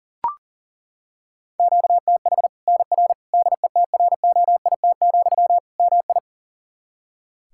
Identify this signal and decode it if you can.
Morse code, heard as a keyed tone, about 40 wpm, 700 Hz. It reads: QTH NR DETROIT, MI
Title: K1USN SST QSO Practice 40wpm